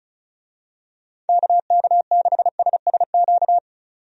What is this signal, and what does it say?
Morse code, heard as a keyed tone, about 36 wpm, 700 Hz. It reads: KK6SSQ
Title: USA Call Signs - 6 Characters - 35wpm